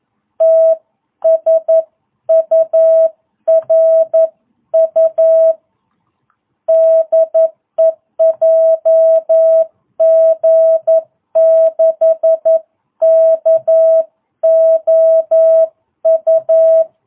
Morse code, heard as a keyed tone, about 11 wpm, 660 Hz. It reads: TSURU DE JG6KOU